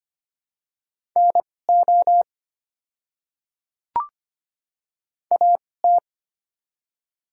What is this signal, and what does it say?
Morse code, heard as a keyed tone, about 25 wpm, spaced as for 18 wpm, 700 Hz. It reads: NO AT
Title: 2x Character Spacing - Top 100 Words - 25wpm